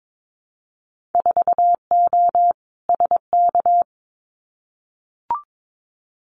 Morse code, heard as a keyed tone, about 22 wpm, 700 Hz. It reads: 4O SK
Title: Winter Field Day Exchanges Encore 22wpm